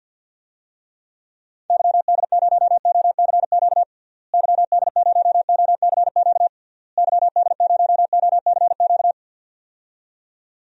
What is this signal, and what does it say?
Morse code, heard as a keyed tone, about 50 wpm, 700 Hz. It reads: YD0OCQ YD0OCQ YD0OCQ
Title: DXCC-100 Call Signs - 6 Characters - Repeated 3x - 50wpm